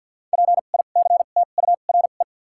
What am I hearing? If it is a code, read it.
Morse code, heard as a keyed tone, about 53 wpm, spaced as for 37 wpm, 700 Hz. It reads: PICTURE
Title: 2x Character Spacing - Top 200 Words - 50wpm